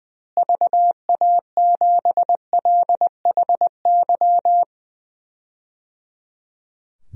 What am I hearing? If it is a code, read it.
Morse code, heard as a keyed tone, about 20 wpm, 700 Hz. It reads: VA7LHY
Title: DXCC-100 Call Signs - 6 Characters - Repeated 3x - 20wpm